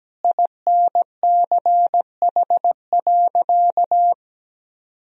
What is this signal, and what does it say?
Morse code, heard as a keyed tone, about 17 wpm, 700 Hz. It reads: INCH.
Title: Sentences from Top 1000 Words - Review Part 3 - 17wpm